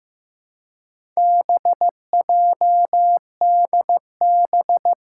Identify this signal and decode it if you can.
Morse code, heard as a keyed tone, about 15 wpm, 700 Hz. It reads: BJDB
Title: Four Challenging Letter-Number Combinations 15wpm